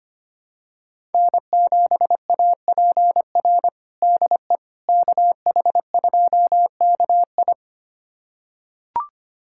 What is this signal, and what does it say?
Morse code, heard as a keyed tone, about 25 wpm, 700 Hz. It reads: N7APR DE KH2KI
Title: USA Callsign DE Callsign - 5 Characters - 25wpm